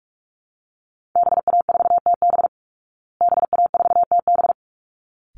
Morse code, heard as a keyed tone, about 44 wpm, 700 Hz. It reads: BA4TB BA4TB
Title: Call Signs Repeated 3x - 45wpm